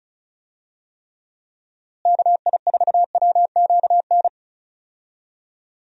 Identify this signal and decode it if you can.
Morse code, heard as a keyed tone, about 36 wpm, 700 Hz. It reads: KI4WQN
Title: USA Call Signs - 6 Characters - Repeated 2x - 35wpm